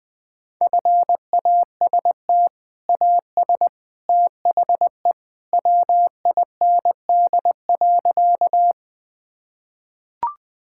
Morse code, heard as a keyed tone, about 20 wpm, 700 Hz. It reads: FAST AS THE WIND.
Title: Sentences with 2-Word Phrases - 20wpm